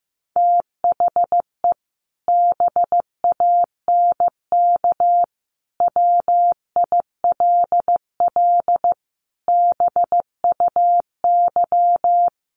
Morse code, heard as a keyed tone, about 15 wpm, 700 Hz. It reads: THE BANK WILL BUY
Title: Sentences from Top 1500 Words 15wpm